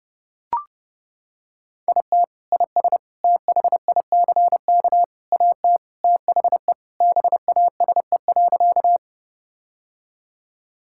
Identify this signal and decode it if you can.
Morse code, heard as a keyed tone, about 32 wpm, 700 Hz. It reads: IT IS THICK AT THE BASE.
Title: Sentences from Top 1600 Words 30wpm